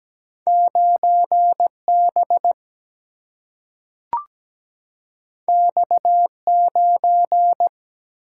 Morse code, heard as a keyed tone, about 17 wpm, 700 Hz. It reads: 9B X9
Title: Single-Letter Single-Number 17wpm